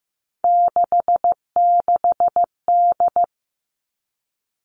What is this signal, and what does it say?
Morse code, heard as a keyed tone, about 15 wpm, 700 Hz. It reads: 66D
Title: Three Challenging Letter-Number Combinations 15wpm